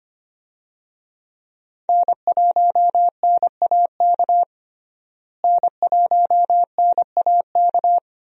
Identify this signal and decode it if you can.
Morse code, heard as a keyed tone, about 25 wpm, 700 Hz. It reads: N1NAK N1NAK
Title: USA Call Signs - 5 Characters - Repeated 2x - 25wpm